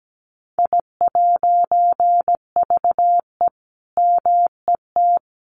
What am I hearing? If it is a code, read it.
Morse code, heard as a keyed tone, about 17 wpm, 700 Hz. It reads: I'VE MET
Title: A Sound Of Thunder 17wpm